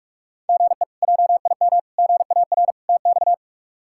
Morse code, heard as a keyed tone, about 46 wpm, 700 Hz. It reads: GE JIM GAR TX
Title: K1USN SST QSO Practice - 45wpm